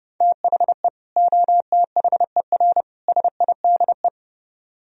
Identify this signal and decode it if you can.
Morse code, heard as a keyed tone, about 34 wpm, 700 Hz. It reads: THE OTHER SIDE